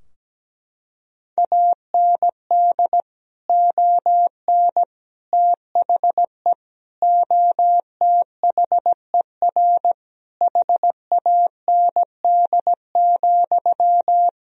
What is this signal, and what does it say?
Morse code, heard as a keyed tone, about 17 wpm, 700 Hz. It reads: AND ON THE OTHER HAND,
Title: Sentences from Top 1000 Words - Review Part 3 - 17wpm